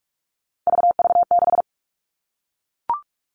Morse code, heard as a keyed tone, about 44 wpm, 700 Hz. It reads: VV6
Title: All Cut Numbers - 3-Digit 45wpm